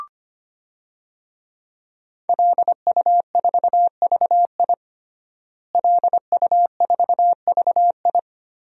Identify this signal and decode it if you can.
Morse code, heard as a keyed tone, about 25 wpm, 700 Hz. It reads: LU4VI LU4VI
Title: DXCC-100 Call Signs - 5 Characters - Repeated 2x - 25wpm